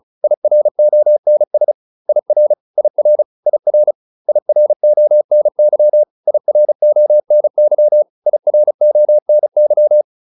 Morse code, heard as a keyed tone, about 34 wpm, 600 Hz. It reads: IRONS IR IR IR IRONY IRONY IRONY